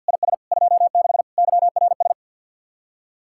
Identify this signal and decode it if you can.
Morse code, heard as a keyed tone, about 62 wpm, 700 Hz. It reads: IS 26 YRS
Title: Speed-Racing Common QSO Phrases - 50wpm